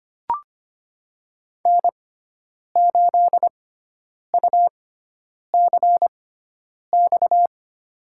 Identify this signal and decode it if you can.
Morse code, heard as a keyed tone, about 25 wpm, spaced as for 9 wpm, 700 Hz. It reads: N8UCX
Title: USA Call Signs - 6x Character Spacing - 25wpm